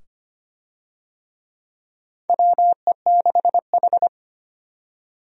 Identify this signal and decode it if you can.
Morse code, heard as a keyed tone, about 25 wpm, 700 Hz. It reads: WE6H
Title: USA Call Signs - 4 Characters - 25wpm